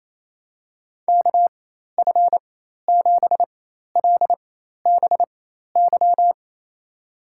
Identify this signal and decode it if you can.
Morse code, heard as a keyed tone, about 28 wpm, spaced as for 13 wpm, 700 Hz. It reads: KF7LBY
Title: USA Call Signs - 4x Character Spacing - 28wpm